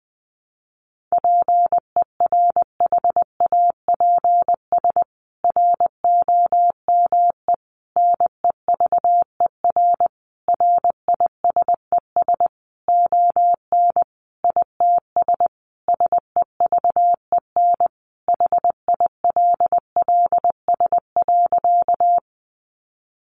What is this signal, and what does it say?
Morse code, heard as a keyed tone, about 20 wpm, 700 Hz. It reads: PERHAPS ROME NEVER RISES ON ITS SEVEN HILLS.
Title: A Sound Of Thunder 20wpm